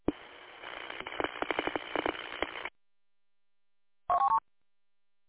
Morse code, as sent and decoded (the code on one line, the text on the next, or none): none